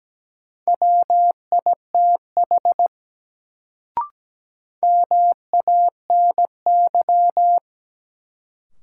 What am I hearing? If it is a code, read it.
Morse code, heard as a keyed tone, about 17 wpm, 700 Hz. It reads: WITH MANY
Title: Sentences from Top 100 Words - New Word Drill - 17wpm